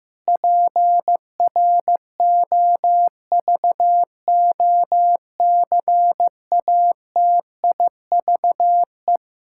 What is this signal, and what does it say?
Morse code, heard as a keyed tone, about 15 wpm, 700 Hz. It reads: PROVOCATIVE
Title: Pangram Challenge v2 - 15wpm